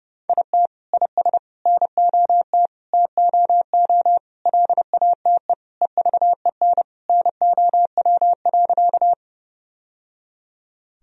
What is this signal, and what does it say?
Morse code, heard as a keyed tone, about 32 wpm, 700 Hz. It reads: IT IS NOT TOO LATE EVEN NOW.
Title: Sentences from Top 1000 Words - Review Part 1 - 30wpm